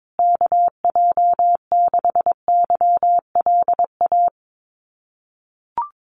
Morse code, heard as a keyed tone, about 22 wpm, 700 Hz. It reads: KJ6YLA
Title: USA Call Signs Repeated 2x 22wpm